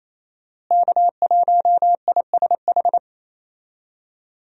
Morse code, heard as a keyed tone, about 28 wpm, 700 Hz. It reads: K1ISH
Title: USA Call Signs - 5 Characters - Repeated 3x - 28wpm